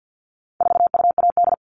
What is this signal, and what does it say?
Morse code, heard as a keyed tone, about 50 wpm, 700 Hz. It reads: 4UAD
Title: All Cut Numbers - 4-Digit 50wpm